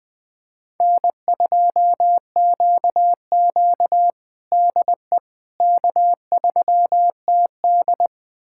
Morse code, heard as a keyed tone, about 20 wpm, 700 Hz. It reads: N2QQ DE K3TD
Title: USA Callsign DE Callsign - 4 Characters - 20wpm